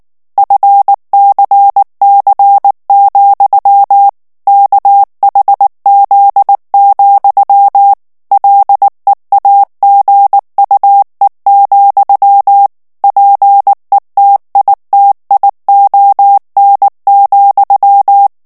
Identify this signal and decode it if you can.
Morse code, heard as a keyed tone, about 19 wpm, 800 Hz. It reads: FCC, KHZ, LEAGUE, PETITION,
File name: morse mp3 / ANGLAIS texte clair 3 vitesse 18 mots minute